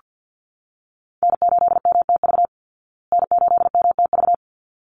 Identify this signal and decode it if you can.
Morse code, heard as a keyed tone, about 48 wpm, 700 Hz. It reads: N8MTV N8MTV
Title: Call Signs Repeated 3x - 50wpm